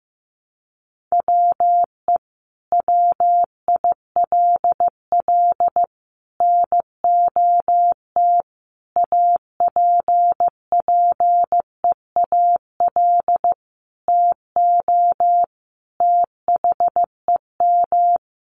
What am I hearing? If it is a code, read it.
Morse code, heard as a keyed tone, about 14 wpm, 700 Hz. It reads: WE WILL NOT APPEAL TO THEM
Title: Sentences from Top 1300 Words 15wpm